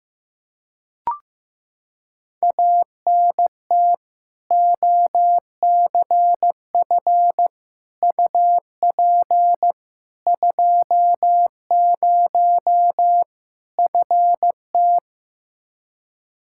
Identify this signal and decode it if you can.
Morse code, heard as a keyed tone, about 15 wpm, 700 Hz. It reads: ANT OCF UP 20 FT
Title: Practice QSO - Exchange 2 - Info - 15wpm